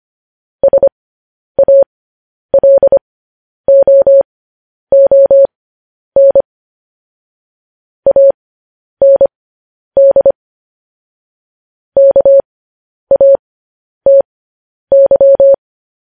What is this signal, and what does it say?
Morse code, heard as a keyed tone, about 25 wpm, spaced as for 10 wpm, 550 Hz. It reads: SALOON AND KATY
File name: podcast / News headlines-1x-25x10